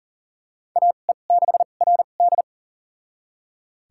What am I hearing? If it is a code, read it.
Morse code, heard as a keyed tone, about 40 wpm, spaced as for 29 wpm, 700 Hz. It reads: AE6RD
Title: USA Call Signs - 5 Characters - 2x Character Spacing - 40wpm